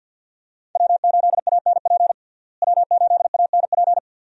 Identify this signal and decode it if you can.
Morse code, heard as a keyed tone, about 50 wpm, 700 Hz. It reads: W8ANP W8ANP
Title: USA Call Signs - 5 Characters - Repeated 2x - 50wpm